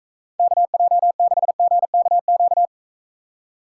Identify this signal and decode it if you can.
Morse code, heard as a keyed tone, about 43 wpm, 700 Hz. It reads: KJ6GKQ